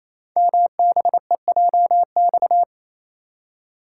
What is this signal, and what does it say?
Morse code, heard as a keyed tone, about 27 wpm, 700 Hz. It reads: MBEJX
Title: Code Groups - 5 Letters - 28wpm